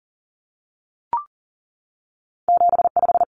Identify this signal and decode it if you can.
Morse code, heard as a keyed tone, about 40 wpm, 700 Hz. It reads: 75
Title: Common Cut Numbers - 2-Digit 40wpm